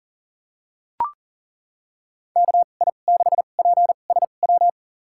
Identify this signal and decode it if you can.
Morse code, heard as a keyed tone, about 40 wpm, spaced as for 29 wpm, 700 Hz. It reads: KI6PSW
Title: USA Call Signs - 6 Characters - 2x Character Spacing - 40wpm